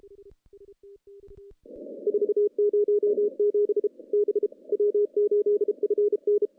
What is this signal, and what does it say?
Morse code, heard as a keyed tone, about 34 wpm, 410 Hz. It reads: 407 B W8FN
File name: CW-SS-k5tr